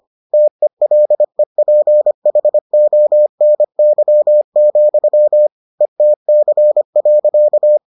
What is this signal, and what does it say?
Morse code, heard as a keyed tone, about 25 wpm, 600 Hz. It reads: TELEPHONY, ETC.